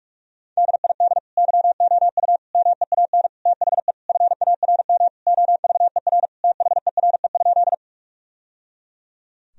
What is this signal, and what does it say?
Morse code, heard as a keyed tone, about 46 wpm, 700 Hz. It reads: DID YOU MEAN THE FARM OVER THERE?
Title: Sentences from Top 1000 Words - Review Part 2 - 45wpm